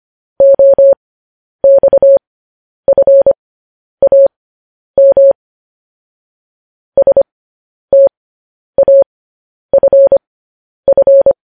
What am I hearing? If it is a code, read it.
Morse code, heard as a keyed tone, about 25 wpm, spaced as for 10 wpm, 550 Hz. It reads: OXFAM STAFF
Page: news-headlines-in-morse-code-at-10-wpm